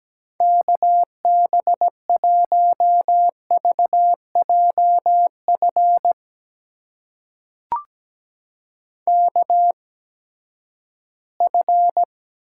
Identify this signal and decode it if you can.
Morse code, heard as a keyed tone, about 17 wpm, spaced as for 12 wpm, 700 Hz. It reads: KB1VJF KF
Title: USA Call Signs - 6 Characters - 8x Character Spacing - 17wpm